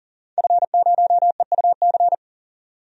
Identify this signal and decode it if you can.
Morse code, heard as a keyed tone, about 40 wpm, 700 Hz. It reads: F0EUC
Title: DXCC-100 Call Signs - 5 Characters - 40wpm